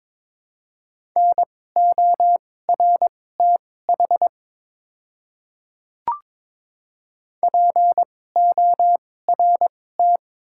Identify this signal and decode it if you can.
Morse code, heard as a keyed tone, about 22 wpm, spaced as for 16 wpm, 700 Hz. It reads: NORTH PORT
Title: 2x Character Spacing - Top 300 Words - 22wpm